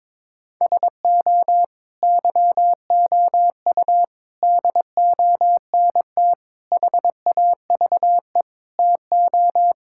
Morse code, heard as a keyed tone, about 22 wpm, 700 Hz. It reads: SO YOU DONT HAVE TO
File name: practice / 22wpm / Sets of 7 Words - 22wpm